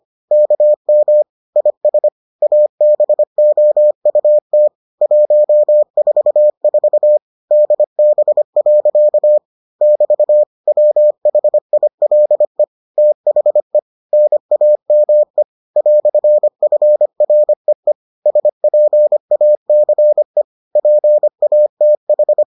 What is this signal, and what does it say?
Morse code, heard as a keyed tone, about 25 wpm, 600 Hz. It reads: KM IS ABOUT 144 DB. <BT> WHILE THE NAME "FREE SPACE PATH